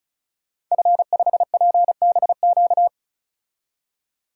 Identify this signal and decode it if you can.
Morse code, heard as a keyed tone, about 34 wpm, 700 Hz. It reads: F5PBQ
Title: DXCC-100 Call Signs - 5 Characters - 35wpm